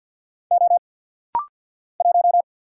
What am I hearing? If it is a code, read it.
Morse code, heard as a keyed tone, about 50 wpm, 700 Hz. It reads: O 1
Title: Single Letter-Number - Rapid-Fire 50wpm